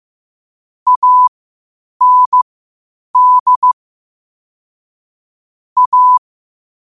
Morse code, heard as a keyed tone, about 15 wpm, spaced as for 8 wpm, 1000 Hz. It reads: AND A
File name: 8wpm